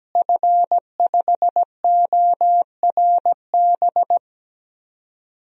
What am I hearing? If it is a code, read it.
Morse code, heard as a keyed tone, about 17 wpm, 700 Hz. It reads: F5ORB
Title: DXCC-100 Call Signs - 5 Characters - Repeated 2x - 17wpm